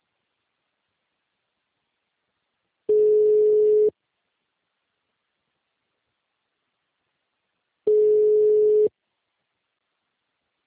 Ringback tone (Europe, 425 Hz).